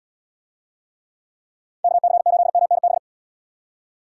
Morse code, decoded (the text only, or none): KC8MNB